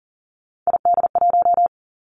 Morse code, heard as a keyed tone, about 38 wpm, 700 Hz. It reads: ID1
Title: Three Challenging Letter-Number Combinations 40wpm